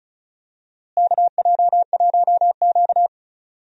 Morse code, heard as a keyed tone, about 36 wpm, 700 Hz. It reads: KJ1Q
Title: USA Call Signs - 4 Characters - Repeated 2x - 35wpm